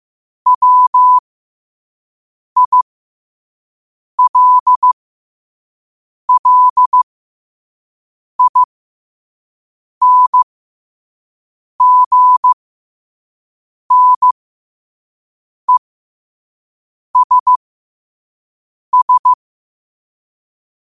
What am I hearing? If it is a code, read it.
Morse code, heard as a keyed tone, about 15 wpm, spaced as for 5 wpm, 1000 Hz. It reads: WILLINGNESS
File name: qotdfiles / 5wpm